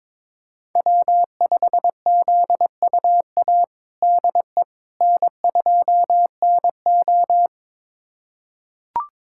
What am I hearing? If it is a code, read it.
Morse code, heard as a keyed tone, about 22 wpm, 700 Hz. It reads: W5ZUA DE N2NO
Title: USA Callsign DE Callsign 22wpm